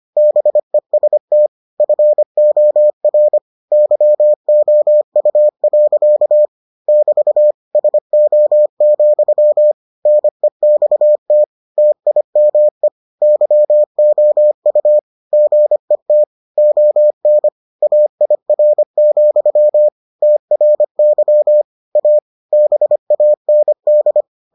Morse code, heard as a keyed tone, about 25 wpm, 600 Hz. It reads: BEST FOR YOU. <BT> SO, NEXT TIME YOU GET ON AIR, TRY A BAND